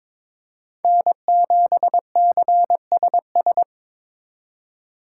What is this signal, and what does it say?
Morse code, heard as a keyed tone, about 25 wpm, 700 Hz. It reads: N7CSS
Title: Speed Racing - USA 1x3 Call Signs 22wpm